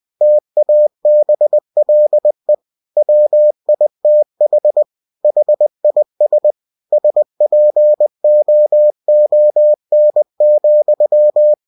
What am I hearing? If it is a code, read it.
Morse code, heard as a keyed tone, about 20 wpm, 600 Hz. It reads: TABLE WITH HIS SPOON,
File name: cw-books / winnie-the-pooh / winnie-20wpm-page-0107